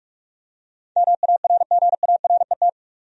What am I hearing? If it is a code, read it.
Morse code, heard as a keyed tone, about 46 wpm, 700 Hz. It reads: MARGARET
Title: USA Common Names - 45wpm